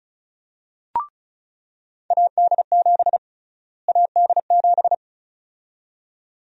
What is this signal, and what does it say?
Morse code, heard as a keyed tone, about 34 wpm, 700 Hz. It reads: AD7 AD7
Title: USA Call Sign Prefixes - Repeated 2x - 35wpm